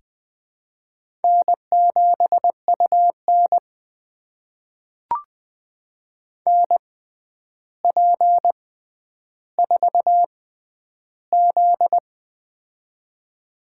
Morse code, heard as a keyed tone, about 20 wpm, spaced as for 10 wpm, 700 Hz. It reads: N7UN NP4Z
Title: USA Call Signs - 4 Characters - 6x Character Spacing - 20wpm